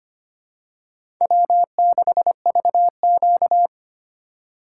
Morse code, heard as a keyed tone, about 25 wpm, 700 Hz. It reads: W6VQ
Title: USA Call Signs - 4 Characters - 2x Character Spacing - 25wpm